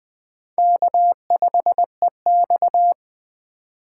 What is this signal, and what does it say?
Morse code, heard as a keyed tone, about 21 wpm, 700 Hz. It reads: K5EX
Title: Speed Racing - USA 1x2 Call Signs 17wpm